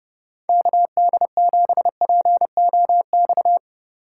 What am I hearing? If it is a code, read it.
Morse code, heard as a keyed tone, about 31 wpm, 700 Hz. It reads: KD7POX
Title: Speed Racing - USA 2x3 Call Signs 25wpm